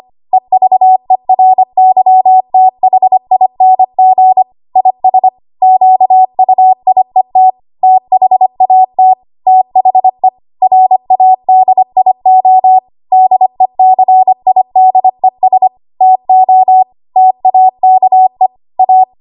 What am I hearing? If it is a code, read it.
Morse code, heard as a keyed tone, about 25 wpm, 750 Hz. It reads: EVERYTHING IS QUIET THAT THE RADIO DECIDES TO TAKE A
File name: Archive / 25 WPM / 25WPM